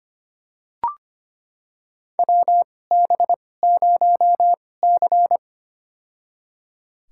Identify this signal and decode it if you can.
Morse code, heard as a keyed tone, about 25 wpm, spaced as for 18 wpm, 700 Hz. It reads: WB0C
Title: USA Call Signs - 4 Characters - 2x Character Spacing - 25wpm